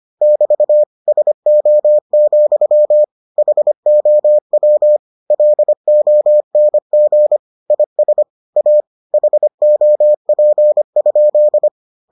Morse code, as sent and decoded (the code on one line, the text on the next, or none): -...- / ... --- --..-- / .... --- .-- / .-.. --- -. --. / .. ... / .- / .... --- .--. ..--..
<BT> SO, HOW LONG IS A HOP?